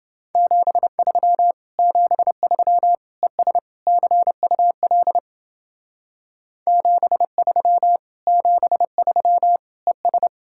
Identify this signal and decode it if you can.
Morse code, heard as a keyed tone, about 29 wpm, 700 Hz. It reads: 73 73 ES CUL 73 73 ES